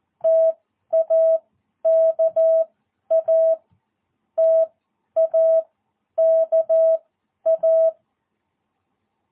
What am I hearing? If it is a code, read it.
Morse code, heard as a keyed tone, about 14 wpm, spaced as for 10 wpm, 660 Hz. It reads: TAKA TAKA
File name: jh8cbh / 2022cw